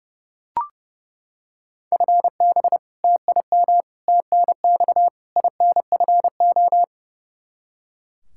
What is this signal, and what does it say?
Morse code, heard as a keyed tone, about 30 wpm, 700 Hz. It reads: FB TIM TNX INFO
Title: Practice QSO - Exchange 3 - More Info - 30wpm